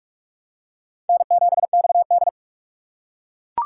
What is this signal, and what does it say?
Morse code, heard as a keyed tone, about 41 wpm, 700 Hz. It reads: N7XD